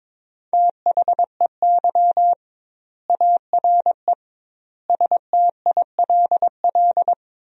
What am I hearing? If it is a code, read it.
Morse code, heard as a keyed tone, about 22 wpm, 700 Hz. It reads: THEY ARE STILL
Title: 2x Word-Spacing - Sentences from Top 400 Words - 22wpm